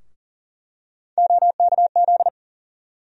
Morse code, heard as a keyed tone, about 37 wpm, 700 Hz. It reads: OKZ
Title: USA Call Sign Postfixes - 3 Letters 40wpm